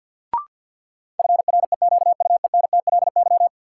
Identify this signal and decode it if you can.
Morse code, heard as a keyed tone, about 53 wpm, 700 Hz. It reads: FREQUENTLY
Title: Sentences from Top 1300 Words - New Word Drill - 50wpm